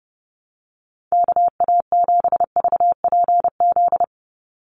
Morse code, heard as a keyed tone, about 30 wpm, 700 Hz. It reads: KA7VPZ
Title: USA Call Signs 30wpm